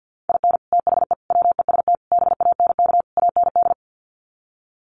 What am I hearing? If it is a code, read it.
Morse code, heard as a keyed tone, about 50 wpm, 700 Hz. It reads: IN THE WEST BANK AND
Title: Sets of 5 Words Encore2 50wpm